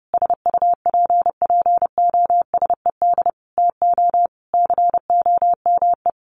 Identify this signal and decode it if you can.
Morse code, heard as a keyed tone, about 30 wpm, 700 Hz. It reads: SUPPOSED TO COME
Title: Additional Easy Phrases 30wpm